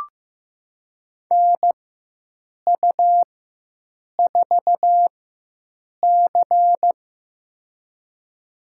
Morse code, heard as a keyed tone, about 15 wpm, spaced as for 7 wpm, 700 Hz. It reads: NU4C
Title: USA Call Signs - 4x Character Spacing - 15wpm